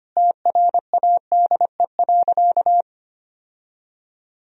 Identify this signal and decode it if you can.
Morse code, heard as a keyed tone, about 25 wpm, 700 Hz. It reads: TRADE.
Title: Sentences from Top 1000 Words - Review Part 4 - 25wpm